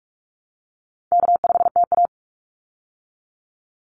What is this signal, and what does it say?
Morse code, heard as a keyed tone, about 44 wpm, 700 Hz. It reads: K5TA
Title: Easy Call Signs - 45wpm